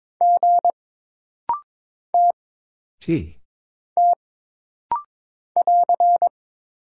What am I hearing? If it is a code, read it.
Morse code, heard as a keyed tone, about 22 wpm, 700 Hz. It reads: G T T <AR>
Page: This rapid-fire timing is geared to push your ICR (Instant Character Recognition) skills.